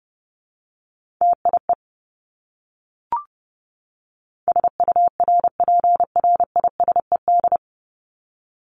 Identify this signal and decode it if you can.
Morse code, heard as a keyed tone, about 30 wpm, 700 Hz. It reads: TIE SURPRISED
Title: Top 1501-1750 Words 30wpm